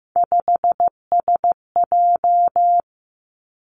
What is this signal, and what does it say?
Morse code, heard as a keyed tone, about 15 wpm, 700 Hz. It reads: 5SJ
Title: Three Challenging Letter-Number Combinations 15wpm